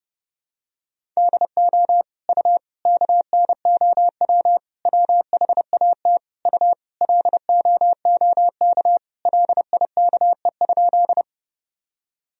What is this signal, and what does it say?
Morse code, heard as a keyed tone, about 30 wpm, 700 Hz. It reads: DO U KNOW WHAT U LOOK LIKE?
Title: Sentences from Top 100 Words and Abbreviations 30wpm